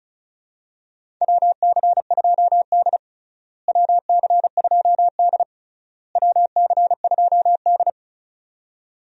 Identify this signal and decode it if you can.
Morse code, heard as a keyed tone, about 34 wpm, 700 Hz. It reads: WC2D WC2D WC2D
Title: USA Call Signs - 4 Characters - Repeated 3x - 35wpm